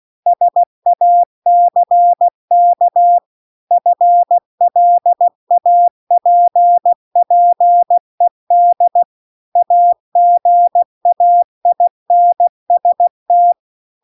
Morse code, heard as a keyed tone, about 16 wpm, 700 Hz. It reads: SACK FLAPPED AGAINST